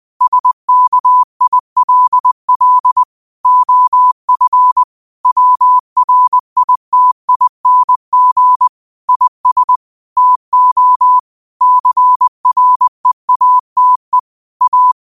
Morse code, heard as a keyed tone, about 20 wpm, 1000 Hz. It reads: SKILL OF WRITING IS TO CREATE A